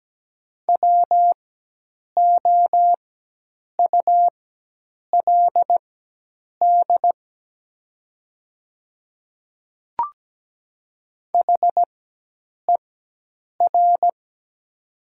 Morse code, heard as a keyed tone, about 17 wpm, spaced as for 8 wpm, 700 Hz. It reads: WOULD HER
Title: 4x Character Spacing - Top 200 Words - 17wpm